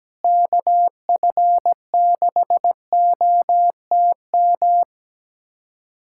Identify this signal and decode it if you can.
Morse code, heard as a keyed tone, about 17 wpm, 700 Hz. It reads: KF6OTM
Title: USA Callsign DE Callsign - 6 Characters - 17wpm